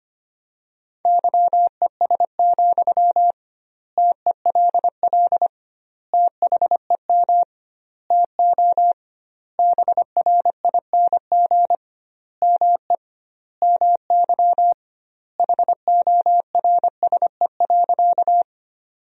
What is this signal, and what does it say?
Morse code, heard as a keyed tone, about 25 wpm, 700 Hz. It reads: YES, TELL THEM TO BRING ME MY HORSE.
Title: 2x Word-Spacing - Sentences from Top 500 Words - 25wpm